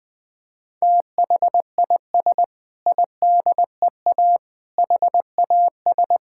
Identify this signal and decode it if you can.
Morse code, heard as a keyed tone, about 19 wpm, 700 Hz. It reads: THIS IDEA HAS
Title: Sentences with 6-Word Phrases - 20wpm